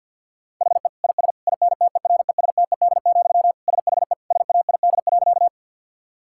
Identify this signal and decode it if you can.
Morse code, heard as a keyed tone, about 50 wpm, 700 Hz. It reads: HE IS INTERESTED, SHE SAID.
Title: Sentences from Top 1000 Words - Review Part 6 - 50wpm